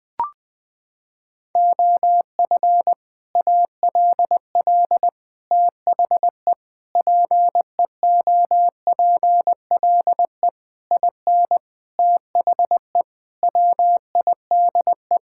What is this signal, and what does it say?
Morse code, heard as a keyed tone, about 20 wpm, 700 Hz. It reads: OF ALL THE PEOPLE IN THE WIDE
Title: Sentences from Top 1000 Words - Review Part 6 - 20wpm